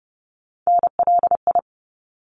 Morse code, heard as a keyed tone, about 30 wpm, 700 Hz. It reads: NLI
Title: ARRL Sections 30wpm